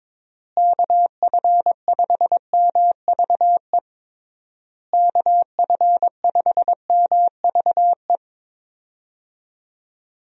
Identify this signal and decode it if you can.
Morse code, heard as a keyed tone, about 22 wpm, 700 Hz. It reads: KF5MVE KF5MVE
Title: USA Call Signs - 6 Characters - Repeated 3x - 22wpm